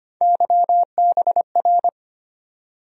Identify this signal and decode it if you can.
Morse code, heard as a keyed tone, about 25 wpm, 700 Hz. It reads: YBR
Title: USA Call Sign Postfixes - 3 Letters 25wpm